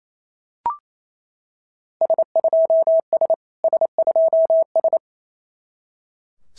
Morse code, heard as a keyed tone, about 27 wpm, 690 Hz. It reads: S2S S2S
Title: SOTA QSO Practice 28wpm